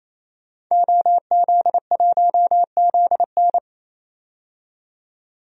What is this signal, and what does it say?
Morse code, heard as a keyed tone, about 28 wpm, 700 Hz. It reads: OZ1ZN
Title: DXCC-100 Call Signs - 5 Characters - 28wpm